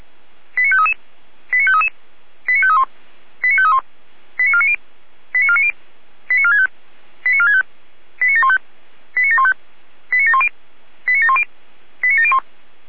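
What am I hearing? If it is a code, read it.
Five-tone selective call (ZVEI), heard as quick runs of five tones, tones of about 70 ms. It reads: 89533 89533 89531 89531 89490 89490 89465 89465 89715 89715 89710 89710 89701
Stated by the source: ZVEI five-tone call 89485, received on SDR.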